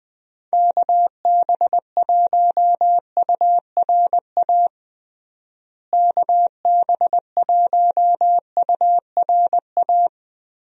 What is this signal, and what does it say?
Morse code, heard as a keyed tone, about 20 wpm, 700 Hz. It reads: KB1URA KB1URA
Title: USA Call Signs - 6 Characters - Repeated 3x - 20wpm